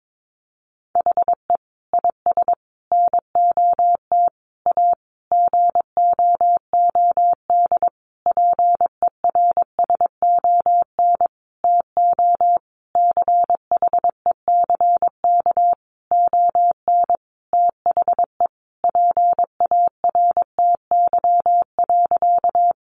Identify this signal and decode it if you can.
Morse code, heard as a keyed tone, about 22 wpm, 700 Hz. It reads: HE IS NOT A GOOD PERSON TO CHECK ON THE PARTY.
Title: Sentences from Top 1400 Words 22wpm